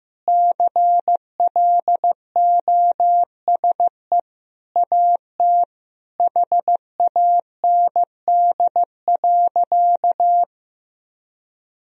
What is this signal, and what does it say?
Morse code, heard as a keyed tone, about 15 wpm, 700 Hz. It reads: CLOSE AT HAND.
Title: Sentences from Top 1000 Words - Review Part 1 - 15wpm